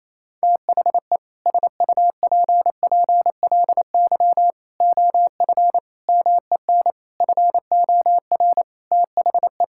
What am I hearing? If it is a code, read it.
Morse code, heard as a keyed tone, about 28 wpm, 700 Hz. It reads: THE SUPPLY OF MEN FOR THE
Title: Sentences from Top 1000 Words - Review Part 6 - 28wpm